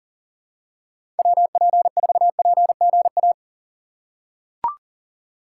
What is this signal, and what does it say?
Morse code, heard as a keyed tone, about 37 wpm, 700 Hz. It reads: WP4PGA
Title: USA Call Signs - 6 Characters - 4x Character Spacing - 40wpm